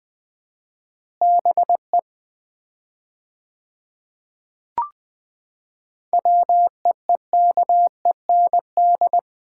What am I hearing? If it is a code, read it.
Morse code, heard as a keyed tone, about 20 wpm, 700 Hz. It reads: BE WEEKEND
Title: 4x Word-Spacing - Sets of 5 Words - 20wpm